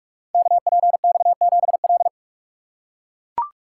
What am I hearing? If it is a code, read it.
Morse code, heard as a keyed tone, about 44 wpm, 700 Hz. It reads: KPX7L
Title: Code Groups - 5 Letter:Numbers - 45wpm